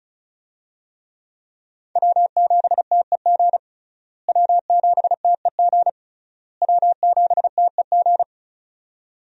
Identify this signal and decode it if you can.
Morse code, heard as a keyed tone, about 36 wpm, 700 Hz. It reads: W7TEG W7TEG W7TEG
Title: USA Call Signs - 5 Characters - Repeated 3x - 35wpm